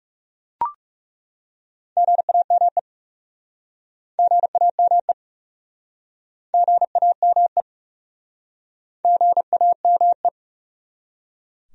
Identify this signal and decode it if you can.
Morse code, heard as a keyed tone, about 39 wpm, 700 Hz. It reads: GAME GAME GAME GAME